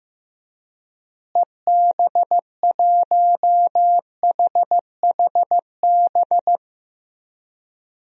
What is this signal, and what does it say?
Morse code, heard as a keyed tone, about 15 wpm, 700 Hz. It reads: EB1HHB
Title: DXCC-100 Call Signs - 6 Characters - Repeated 2x - 15wpm